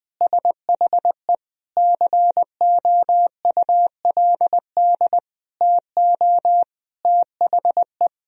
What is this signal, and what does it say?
Morse code, heard as a keyed tone, about 20 wpm, 700 Hz. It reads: SHE COULD TO THE
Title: Sentences from Top 1000 Words - Review Part 4 - 20wpm